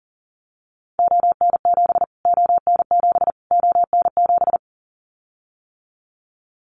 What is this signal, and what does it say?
Morse code, heard as a keyed tone, about 40 wpm, 700 Hz. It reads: ON7 ON7 ON7
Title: Callsign Prefixes for Top 25 DXCC - Repeated 3x - 40wpm